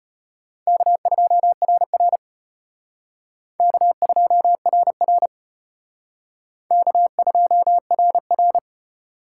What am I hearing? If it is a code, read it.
Morse code, heard as a keyed tone, about 32 wpm, 700 Hz. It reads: K2RR K2RR K2RR